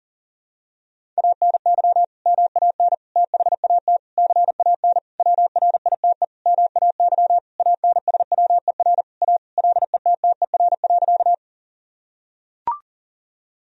Morse code, heard as a keyed tone, about 40 wpm, 700 Hz. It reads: ANY MAN THAT CAN WRITE MAY ANSWER A LETTER.
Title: Sentences from Top 1000 Words - Review Part 1 - 40wpm